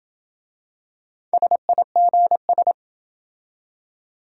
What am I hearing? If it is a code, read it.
Morse code, heard as a keyed tone, about 26 wpm, 700 Hz. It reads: SIGS